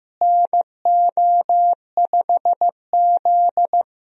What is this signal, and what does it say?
Morse code, heard as a keyed tone, about 16 wpm, 700 Hz. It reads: NO5Z
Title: USA Call Signs - 4 Characters - Repeated 3x - 15wpm